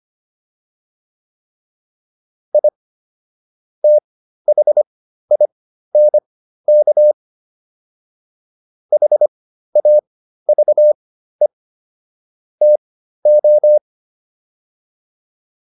Morse code, heard as a keyed tone, about 25 wpm, spaced as for 13 wpm, 600 Hz. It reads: I THINK HAVE TO